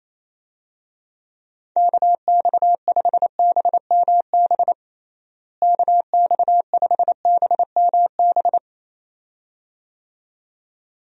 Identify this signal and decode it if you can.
Morse code, heard as a keyed tone, about 28 wpm, 700 Hz. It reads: KX5BMB KX5BMB
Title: USA Call Signs - 6 Characters - Repeated 2x - 28wpm